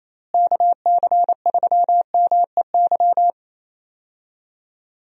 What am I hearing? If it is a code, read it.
Morse code, heard as a keyed tone, about 28 wpm, 700 Hz. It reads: KC3MEY
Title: USA Call Signs - 6 Characters - Repeated 3x - 28wpm